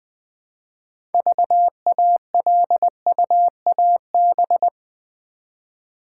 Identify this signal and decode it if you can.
Morse code, heard as a keyed tone, about 20 wpm, 700 Hz. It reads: VALUAB
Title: Build A Word - 2 Letters At A Time - 8-Letter Words - 20wpm